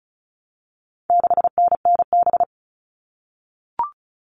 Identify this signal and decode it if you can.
Morse code, heard as a keyed tone, about 34 wpm, 700 Hz. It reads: 6NNB
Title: All Cut Numbers - 4-Digit 35wpm